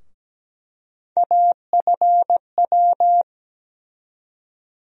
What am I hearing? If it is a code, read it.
Morse code, heard as a keyed tone, about 17 wpm, 700 Hz. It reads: AFW
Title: USA Call Sign Postfixes - 3 Letters 17wpm